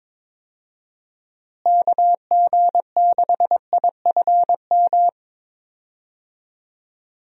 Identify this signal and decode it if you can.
Morse code, heard as a keyed tone, about 22 wpm, 700 Hz. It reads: KG6IFM
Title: USA Call Signs - 6 Characters - Repeated 2x - 22wpm